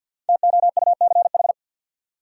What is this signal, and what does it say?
Morse code, heard as a keyed tone, about 52 wpm, 700 Hz. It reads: TOUCH
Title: Sentences from Top 1200 Words - New Word Drill - 50wpm